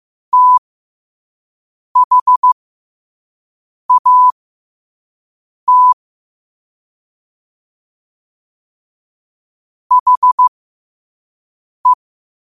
Morse code, heard as a keyed tone, about 15 wpm, spaced as for 5 wpm, 1000 Hz. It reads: THAT HE
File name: qotdfiles / 5wpm